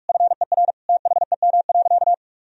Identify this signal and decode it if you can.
Morse code, heard as a keyed tone, about 43 wpm, 700 Hz. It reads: FER THEM.
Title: Sentences from Top 300 Words and Abbreviations 45wpm